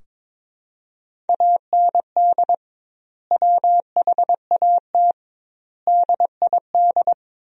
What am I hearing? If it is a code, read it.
Morse code, heard as a keyed tone, about 21 wpm, 700 Hz. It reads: AND WHAT DID
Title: 2x Word-Spacing - Sentences from Top 300 Words - 22wpm